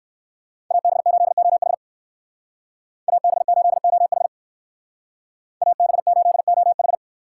Morse code, heard as a keyed tone, about 56 wpm, 700 Hz. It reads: AB8OH AB8OH AB8OH
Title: Speed Racing - USA 2x2 Call Signs 45wpm